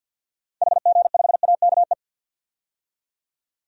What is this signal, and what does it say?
Morse code, heard as a keyed tone, about 46 wpm, 700 Hz. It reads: HG5AKE
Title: DXCC-100 Call Signs - 6 Characters - 50wpm